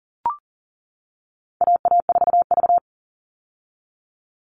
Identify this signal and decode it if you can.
Morse code, heard as a keyed tone, about 39 wpm, 700 Hz. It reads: AA4V
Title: All Cut Numbers - 4-Digit 40wpm